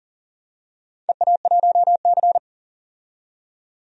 Morse code, heard as a keyed tone, about 43 wpm, 700 Hz. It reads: EA1C
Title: DXCC-100 Call Signs - 4 Characters - 40wpm